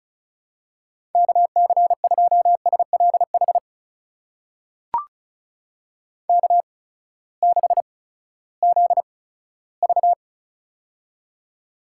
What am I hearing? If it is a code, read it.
Morse code, heard as a keyed tone, about 36 wpm, spaced as for 18 wpm, 700 Hz. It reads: KC2SLH K6ZV
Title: USA Call Signs - 8x Character Spacing - 35wpm